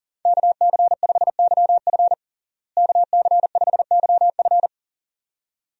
Morse code, heard as a keyed tone, about 40 wpm, 700 Hz. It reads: KC5YF KC5YF
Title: USA Call Signs - 5 Characters - Repeated 2x - 40wpm